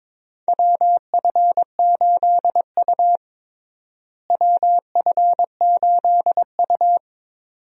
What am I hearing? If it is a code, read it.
Morse code, heard as a keyed tone, about 21 wpm, 700 Hz. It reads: WF8U WF8U
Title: USA Call Signs - 4 Characters - Repeated 2x - 22wpm